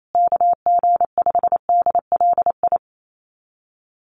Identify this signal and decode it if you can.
Morse code, heard as a keyed tone, about 28 wpm, 700 Hz. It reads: KG5DLI
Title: USA Call Signs Repeated 3x 28wpm